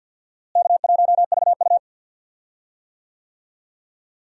Morse code, heard as a keyed tone, about 53 wpm, 700 Hz. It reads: K1VU
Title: USA Call Signs - 4 Characters - Repeated 3x - 50wpm